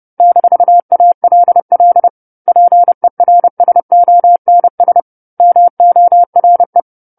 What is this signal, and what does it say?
Morse code, heard as a keyed tone, about 30 wpm, 700 Hz. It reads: -ALL PERSONS MORE
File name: 10-Chapter-10 30 WPM